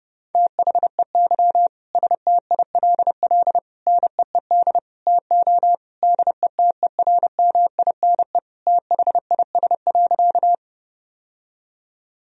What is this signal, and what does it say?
Morse code, heard as a keyed tone, about 30 wpm, 700 Hz. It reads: THEY STILL NEED TO DETERMINE THIS.
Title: Sentences from Top 1000 Words - Review Part 4 - 30wpm